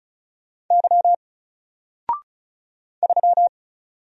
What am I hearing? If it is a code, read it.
Morse code, heard as a keyed tone, about 34 wpm, 700 Hz. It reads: Y 3
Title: Single Letter-Number 35wpm